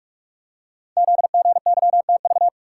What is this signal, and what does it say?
Morse code, heard as a keyed tone, about 44 wpm, 690 Hz. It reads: ZGYTV
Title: Code Groups - 5 Letters - 45wpm